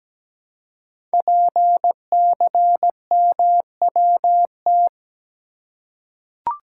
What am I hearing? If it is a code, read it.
Morse code, heard as a keyed tone, about 17 wpm, 700 Hz. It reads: PCMWT
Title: Code Groups - 5 Letters - 17wpm